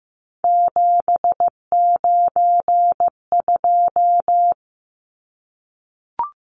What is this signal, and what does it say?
Morse code, heard as a keyed tone, about 15 wpm, 700 Hz. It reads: 792
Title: Numbers - 3-Digit 15wpm